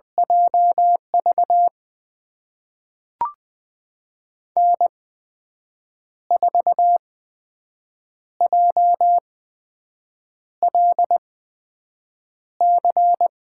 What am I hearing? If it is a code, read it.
Morse code, heard as a keyed tone, about 20 wpm, spaced as for 6 wpm, 700 Hz. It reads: JV N4JLC
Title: USA Call Signs - 5 Characters - 8x Character Spacing - 20wpm